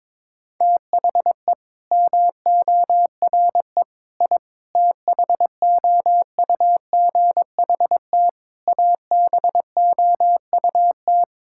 Morse code, heard as a keyed tone, about 22 wpm, 700 Hz. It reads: THE MORE I THOUGHT ABOUT
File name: Sets of 9 Words - 22wpm